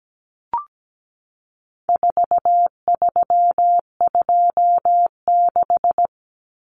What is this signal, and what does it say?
Morse code, heard as a keyed tone, about 17 wpm, 700 Hz. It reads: 4326
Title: Common Cut Numbers - 4-Digit 17wpm